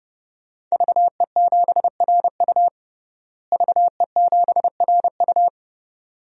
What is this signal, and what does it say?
Morse code, heard as a keyed tone, about 32 wpm, 700 Hz. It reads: VE7RU VE7RU
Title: DXCC-100 Call Signs - 5 Characters - Repeated 2x - 30wpm